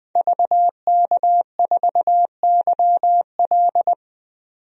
Morse code, heard as a keyed tone, about 20 wpm, 700 Hz. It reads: VK4YL
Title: DXCC-100 Call Signs - 5 Characters - 20wpm